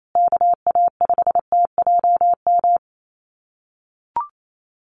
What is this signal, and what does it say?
Morse code, heard as a keyed tone, about 28 wpm, 700 Hz. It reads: KA5TJM
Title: USA Call Signs Repeated 3x 28wpm